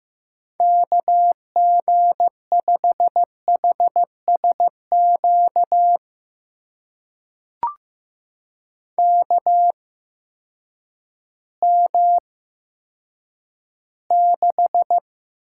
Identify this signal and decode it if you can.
Morse code, heard as a keyed tone, about 15 wpm, spaced as for 9 wpm, 700 Hz. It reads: KG5HSQ KM6
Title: USA Call Signs - 6 Characters - 8x Character Spacing - 15wpm